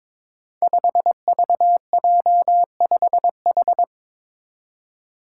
Morse code, heard as a keyed tone, about 22 wpm, 700 Hz. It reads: 5VJ5H
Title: Five Challenging Letter-Number Combinations 22wpm